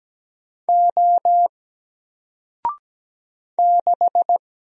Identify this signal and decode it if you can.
Morse code, heard as a keyed tone, about 17 wpm, 700 Hz. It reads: O 6
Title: Single Character (Letter:Number:Punctuation:Prosign) Alt 17wpm